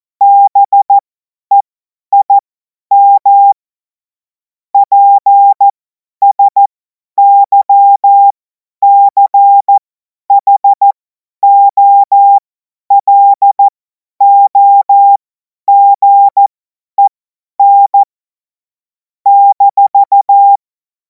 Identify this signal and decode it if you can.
Morse code, heard as a keyed tone, about 14 wpm, spaced as for 10 wpm, 800 Hz. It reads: BEIM PSYCHOLOGEN -
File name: 50 F70